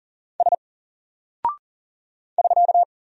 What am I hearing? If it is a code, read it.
Morse code, heard as a keyed tone, about 40 wpm, 700 Hz. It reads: S <SK>
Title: Single Character (Letter:Number:Punctuation:Prosign) Alt 40wpm